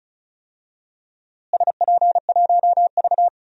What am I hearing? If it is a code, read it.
Morse code, heard as a keyed tone, about 36 wpm, 700 Hz. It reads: SP1V
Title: DXCC-100 Call Signs - 4 Characters - Repeated 2x - 35wpm